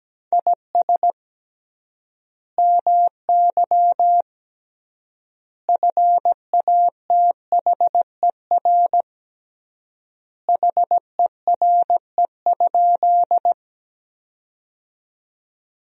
Morse code, heard as a keyed tone, about 17 wpm, 700 Hz. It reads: IS MY FATHER HERE?
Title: 3x Word-Spacing - Sentences from Top 300 Words - 17wpm